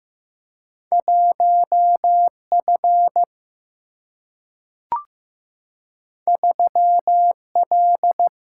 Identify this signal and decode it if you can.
Morse code, heard as a keyed tone, about 15 wpm, 700 Hz. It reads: 1F 3L
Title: Single-Letter Single-Number 15wpm